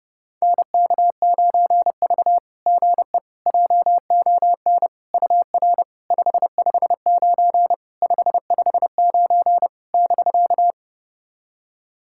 Morse code, heard as a keyed tone, about 30 wpm, 690 Hz. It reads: NK9V GE JON UR 559 559 <BK>
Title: SOTA QSO Practice 30wpm